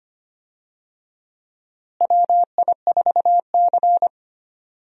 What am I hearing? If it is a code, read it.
Morse code, heard as a keyed tone, about 25 wpm, 700 Hz. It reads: WI4C
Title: USA Call Signs - 4 Characters - Repeated 3x - 25wpm